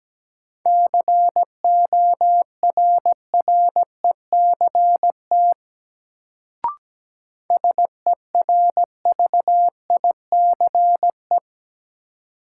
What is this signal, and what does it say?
Morse code, heard as a keyed tone, about 17 wpm, 700 Hz. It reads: CORRECT SERVICE
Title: Sentences from Top 700 Words - New Word Drill - 17wpm